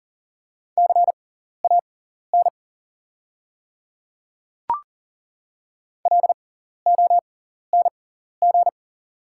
Morse code, heard as a keyed tone, about 42 wpm, spaced as for 14 wpm, 690 Hz. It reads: CAN LONG
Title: 6x Character Spacing - Top 200 Words - 40wpm